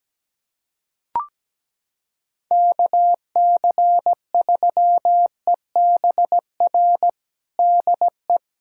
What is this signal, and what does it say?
Morse code, heard as a keyed tone, about 17 wpm, 700 Hz. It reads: KC3EBR DE
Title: USA Callsign DE Callsign - 6 Characters - 17wpm